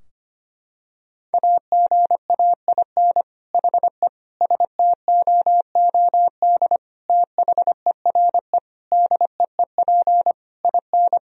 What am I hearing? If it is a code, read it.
Morse code, heard as a keyed tone, about 25 wpm, 700 Hz. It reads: AGAIN HE STOOD THERE DEEP IN
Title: Sentences from Top 1000 Words - Review Part 3 - 25wpm